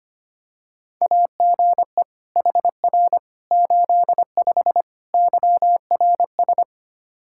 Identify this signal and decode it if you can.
Morse code, heard as a keyed tone, about 25 wpm, 700 Hz. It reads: AGE HR 85 YRS
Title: Practice QSO - Exchange 3 - More Info - 25wpm